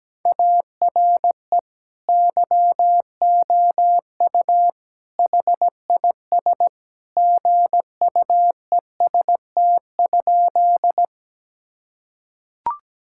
Morse code, heard as a keyed tone, about 17 wpm, 700 Hz. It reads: ARE YOU HIS GUEST?
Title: Sentences from Top 1600 Words 17wpm